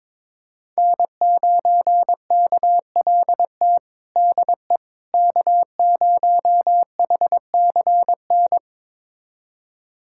Morse code, heard as a keyed tone, about 22 wpm, 700 Hz. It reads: N9KLT DE K0HCN
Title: USA Callsign DE Callsign - 5 Characters - 22wpm